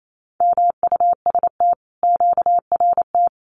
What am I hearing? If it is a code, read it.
Morse code, heard as a keyed tone, about 28 wpm, 700 Hz. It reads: MUST QRT
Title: Common QSO Phrases Encore 28wpm